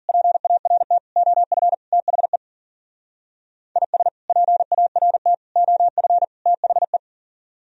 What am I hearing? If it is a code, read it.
Morse code, heard as a keyed tone, about 50 wpm, 700 Hz. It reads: PART OF THE IS PART OF THE
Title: Speed-Racing - Sets of 4 Words 40wpm